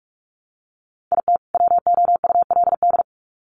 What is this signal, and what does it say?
Morse code, heard as a keyed tone, about 44 wpm, 700 Hz. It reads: IT WOULD
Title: Sets of 2 Words 45wpm